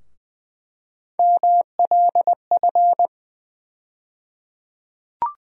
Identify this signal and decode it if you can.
Morse code, heard as a keyed tone, about 20 wpm, 700 Hz. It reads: MLF
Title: USA Call Sign Postfixes - 3 Letters 20wpm